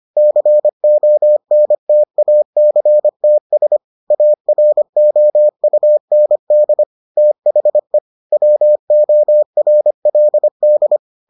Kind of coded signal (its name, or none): Morse code